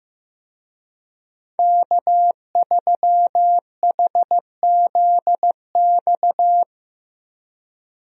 Morse code, heard as a keyed tone, about 15 wpm, 700 Hz. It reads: K3HZX
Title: USA Call Signs - 5 Characters - Repeated 2x - 15wpm